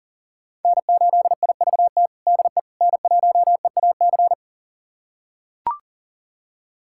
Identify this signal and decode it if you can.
Morse code, heard as a keyed tone, about 40 wpm, 700 Hz. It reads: N8IVT DE N1EAC
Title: USA Callsign DE Callsign - 5 Characters - 40wpm